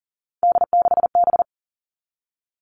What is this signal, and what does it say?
Morse code, heard as a keyed tone, about 41 wpm, 700 Hz. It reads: D6B
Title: Three Challenging Letter-Number Combinations 40wpm